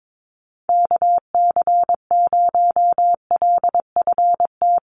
Morse code, heard as a keyed tone, about 22 wpm, 700 Hz. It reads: KC0LFT
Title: USA Call Signs Repeated 2x 22wpm